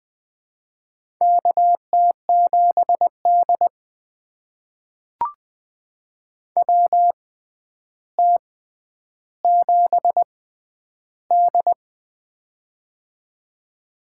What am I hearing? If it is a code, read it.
Morse code, heard as a keyed tone, about 20 wpm, spaced as for 10 wpm, 700 Hz. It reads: KT7D WT7D
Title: USA Call Signs - 4 Characters - 6x Character Spacing - 20wpm